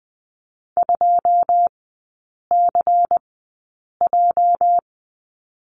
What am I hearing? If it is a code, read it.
Morse code, heard as a keyed tone, about 20 wpm, spaced as for 8 wpm, 700 Hz. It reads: 2CJ
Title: 218 Lesson 35e - Number 0 - Three Characters 20wpm